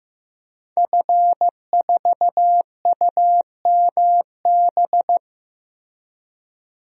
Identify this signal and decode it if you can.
Morse code, heard as a keyed tone, about 15 wpm, 700 Hz. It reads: F4UMB
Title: DXCC-100 Call Signs - 5 Characters - Repeated 3x - 15wpm